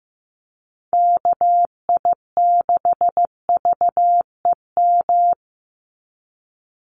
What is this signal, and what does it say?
Morse code, heard as a keyed tone, about 15 wpm, 700 Hz. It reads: KI6VEM
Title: USA Call Signs 15wpm